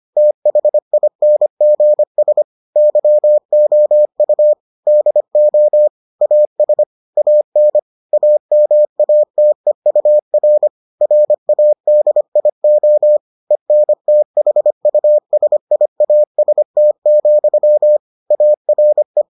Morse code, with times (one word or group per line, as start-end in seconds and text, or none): 0.2-2.4 THINGS
2.8-4.5 YOU
4.9-5.9 DO
6.2-6.8 AS
7.2-7.8 AN
8.1-10.7 AMATEUR
11.0-13.2 RADIO
13.5-18.0 ENTHUSIAST,
18.3-19.2 ARE